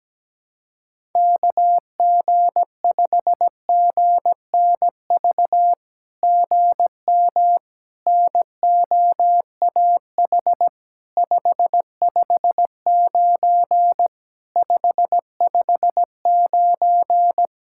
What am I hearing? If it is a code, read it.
Morse code, heard as a keyed tone, about 17 wpm, 700 Hz. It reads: KG5GNV GM NOAH 559 559
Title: SOTA QSO Practice 17wpm